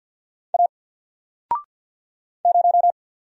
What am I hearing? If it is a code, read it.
Morse code, heard as a keyed tone, about 48 wpm, 700 Hz. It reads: A 0
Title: Single Letter-Number Alt 50wpm